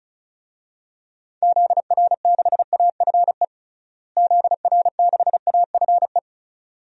Morse code, heard as a keyed tone, about 36 wpm, 700 Hz. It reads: ZR6AFE ZR6AFE
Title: DXCC-100 Call Signs - 6 Characters - Repeated 2x - 35wpm